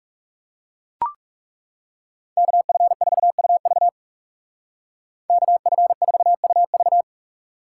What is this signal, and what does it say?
Morse code, heard as a keyed tone, about 35 wpm, 700 Hz. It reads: KF4UV KF4UV
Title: Speed Racing - USA 2x2 Call Signs 30wpm